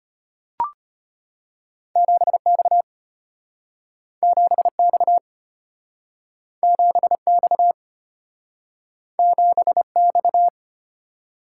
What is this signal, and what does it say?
Morse code, heard as a keyed tone, about 34 wpm, 700 Hz. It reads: 7X 7X 7X 7X